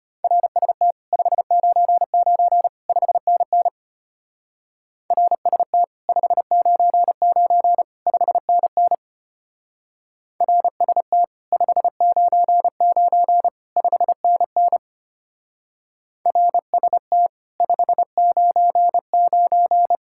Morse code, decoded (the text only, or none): RST 599 5NN RST 599 5NN RST 599 5NN RST 599